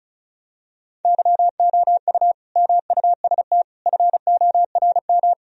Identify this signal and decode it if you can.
Morse code, heard as a keyed tone, about 36 wpm, 700 Hz. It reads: YOU MUST FORM
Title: Sentences from Top 1000 Words - Review Part 1 - 35wpm